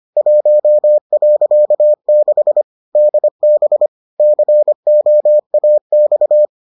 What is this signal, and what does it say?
Morse code, heard as a keyed tone, about 25 wpm, 600 Hz. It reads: 1.6 DB COAX